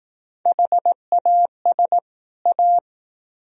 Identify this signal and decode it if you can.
Morse code, heard as a keyed tone, about 18 wpm, 700 Hz. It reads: HAS A
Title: Speed-Racing - Sets of 2 Words 15wpm